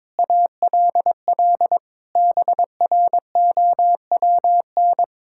Morse code, heard as a keyed tone, about 22 wpm, 700 Hz. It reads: ALL BROWN
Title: Pangram Challenge v2 - 22wpm